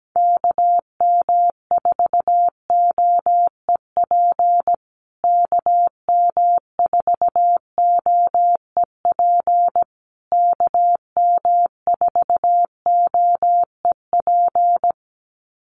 Morse code, heard as a keyed tone, about 17 wpm, 700 Hz. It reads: KM4OEP KM4OEP KM4OEP
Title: USA Call Signs Repeated 3x 17wpm